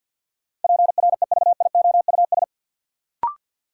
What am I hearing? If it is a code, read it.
Morse code, heard as a keyed tone, about 53 wpm, 700 Hz. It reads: PREVIOUS
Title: Sentences from Top 1500 Words - New Word Drill Alt - 50wpm